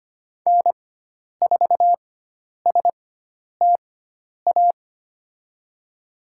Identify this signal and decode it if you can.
Morse code, heard as a keyed tone, about 25 wpm, spaced as for 10 wpm, 700 Hz. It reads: N4STA
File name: Lesson 16g - Letter c - Call Signs 25wpm